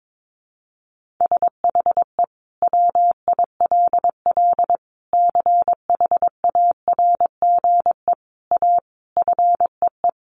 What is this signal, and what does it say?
Morse code, heard as a keyed tone, about 22 wpm, 700 Hz. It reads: SHE WILL CHARGE A FEE
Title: Sentences from Top 1500 Words 22wpm